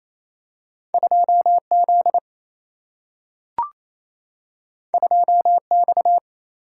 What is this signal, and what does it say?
Morse code, heard as a keyed tone, about 28 wpm, 700 Hz. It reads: 2Z 2X
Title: Single-Letter Single-Number 28wpm